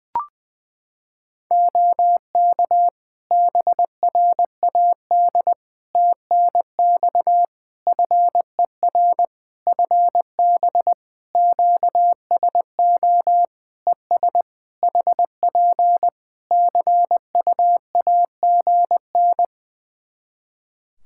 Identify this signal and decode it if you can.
Morse code, heard as a keyed tone, about 20 wpm, 700 Hz. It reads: OK BRAD TNX FER FB QSO ES HP CUAGN
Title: K1USN SST QSO Practice 20wpm